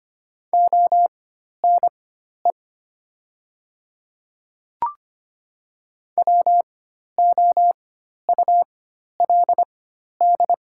Morse code, heard as a keyed tone, about 25 wpm, spaced as for 12 wpm, 700 Hz. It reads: ONE WOULD
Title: 4x Character Spacing - Top 100 Words - 25wpm